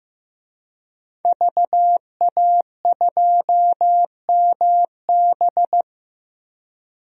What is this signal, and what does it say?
Morse code, heard as a keyed tone, about 15 wpm, 700 Hz. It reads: VA2MB
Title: DXCC-100 Call Signs - 5 Characters - Repeated 3x - 15wpm